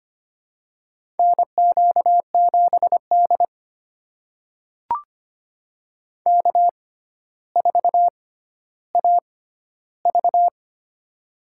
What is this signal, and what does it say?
Morse code, heard as a keyed tone, about 25 wpm, spaced as for 13 wpm, 700 Hz. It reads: NQ7D K4AV
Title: USA Call Signs - 4 Characters - 6x Character Spacing - 25wpm